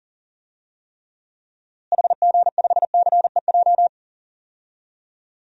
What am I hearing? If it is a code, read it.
Morse code, heard as a keyed tone, about 40 wpm, 700 Hz. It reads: HG5CEJ
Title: DXCC-100 Call Signs - 6 Characters - Repeated 3x - 40wpm